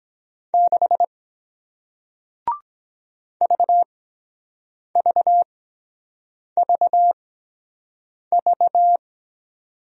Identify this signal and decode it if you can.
Morse code, heard as a keyed tone, about 23 wpm, spaced as for 7 wpm, 700 Hz. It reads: 6 VVVV